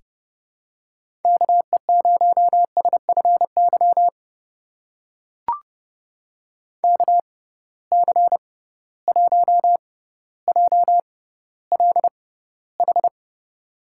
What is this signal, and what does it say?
Morse code, heard as a keyed tone, about 30 wpm, spaced as for 15 wpm, 700 Hz. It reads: KE0SFY KC1JLH
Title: USA Call Signs - 6 Characters - 6x Character Spacing - 30wpm